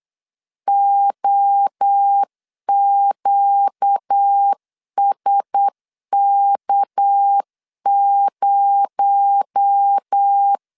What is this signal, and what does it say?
Morse code, heard as a keyed tone, about 8 wpm, 800 Hz. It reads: OQSK0
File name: LEZIONI / 6 PKX0 / 6 PKX0 8wpm 3341